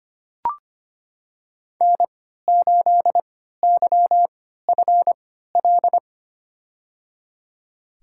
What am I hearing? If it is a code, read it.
Morse code, heard as a keyed tone, about 25 wpm, spaced as for 14 wpm, 700 Hz. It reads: N8YFL
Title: USA Call Signs - 5 Characters - 3x Character Spacing - 25wpm